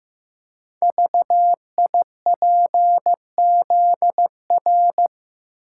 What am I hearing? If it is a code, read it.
Morse code, heard as a keyed tone, about 15 wpm, 700 Hz. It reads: VIPZR
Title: Code Groups - 5 Letters - 15wpm